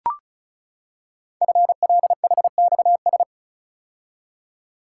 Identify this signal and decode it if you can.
Morse code, heard as a keyed tone, about 36 wpm, 700 Hz. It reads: FLHXS
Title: Code Groups - 5 Letters - 35wpm